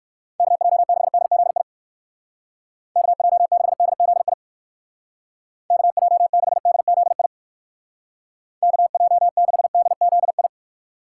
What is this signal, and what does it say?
Morse code, heard as a keyed tone, about 60 wpm, 700 Hz. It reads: KJ6DZI KJ6DZI KJ6DZI KJ6DZI